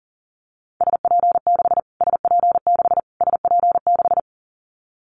Morse code, heard as a keyed tone, about 40 wpm, 700 Hz. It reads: SP6 SP6 SP6
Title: Callsign Prefixes for Top 25 DXCC - Repeated 3x - 40wpm